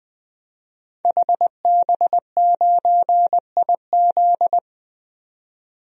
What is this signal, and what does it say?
Morse code, heard as a keyed tone, about 20 wpm, 700 Hz. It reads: HB9IZ
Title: DXCC-100 Call Signs - 5 Characters - 20wpm